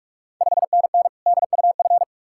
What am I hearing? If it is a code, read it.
Morse code, heard as a keyed tone, about 45 wpm, 700 Hz. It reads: 5NN DUF
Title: Ontario QSO Party Exchange 45wpm